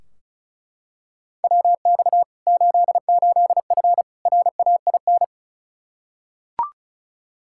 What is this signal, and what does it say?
Morse code, heard as a keyed tone, about 36 wpm, 700 Hz. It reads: WX 88F RAIN
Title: Practice QSO - Exchange 2 - Info - 35wpm